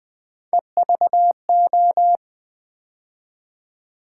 Morse code, heard as a keyed tone, about 20 wpm, 700 Hz. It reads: EVO
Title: Build A Word - 3 Letters At A Time - 9-Letter Words - 20wpm